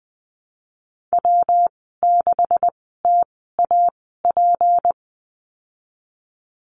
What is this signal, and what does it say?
Morse code, heard as a keyed tone, about 20 wpm, spaced as for 15 wpm, 700 Hz. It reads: W6TAP
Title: 193 Lesson 31g - Letter v - Call Signs 20wpm